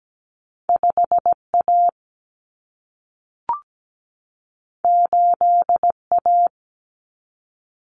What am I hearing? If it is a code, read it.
Morse code, heard as a keyed tone, about 17 wpm, 700 Hz. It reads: 5 A 8 A
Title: Common Cut Numbers - 2-Digit 17wpm